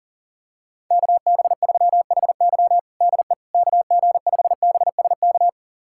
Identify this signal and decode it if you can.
Morse code, heard as a keyed tone, about 40 wpm, 700 Hz. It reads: KB3HY DE KG5BSK
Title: USA Callsign DE Callsign 40wpm